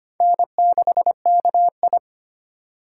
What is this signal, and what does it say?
Morse code, heard as a keyed tone, about 25 wpm, 700 Hz. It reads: N6KI
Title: USA Call Signs - 4 Characters - 2x Character Spacing - 25wpm